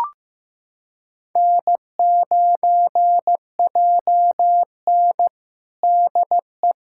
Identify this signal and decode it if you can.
Morse code, heard as a keyed tone, about 15 wpm, 700 Hz. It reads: N9JN DE
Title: USA Callsign DE Callsign - 4 Characters - 15wpm